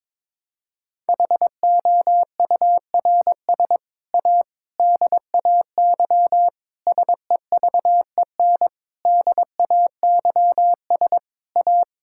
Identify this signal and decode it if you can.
Morse code, heard as a keyed tone, about 22 wpm, 700 Hz. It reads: HOURS A DAY SEVEN DAYS A
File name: practice / 22wpm / Sets of 7 Words - 22wpm